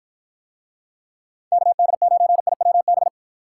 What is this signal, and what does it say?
Morse code, heard as a keyed tone, about 48 wpm, 700 Hz. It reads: KD9IWB